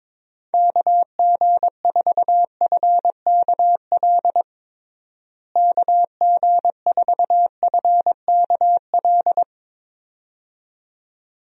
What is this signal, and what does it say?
Morse code, heard as a keyed tone, about 22 wpm, 700 Hz. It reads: KG4FKL KG4FKL
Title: USA Call Signs - 6 Characters - Repeated 2x - 22wpm